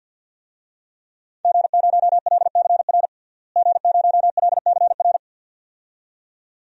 Morse code, heard as a keyed tone, about 52 wpm, 700 Hz. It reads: G0LCR G0LCR
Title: DXCC-100 Call Signs - 5 Characters - Repeated 2x - 50wpm